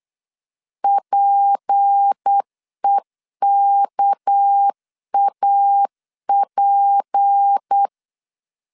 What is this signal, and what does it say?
Morse code, heard as a keyed tone, about 8 wpm, 800 Hz. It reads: PEKAP
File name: LEZIONI / 7 GF9Z / 7 GF9Z 8wpm 3213